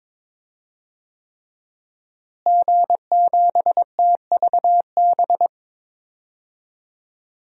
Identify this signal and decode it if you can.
Morse code, heard as a keyed tone, about 22 wpm, 700 Hz. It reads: G7TVB
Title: DXCC-100 Call Signs - 5 Characters - Repeated 3x - 22wpm